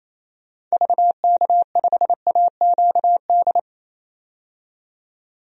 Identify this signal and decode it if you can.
Morse code, heard as a keyed tone, about 28 wpm, 700 Hz. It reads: VK5AQD
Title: DXCC-100 Call Signs - 6 Characters - 28wpm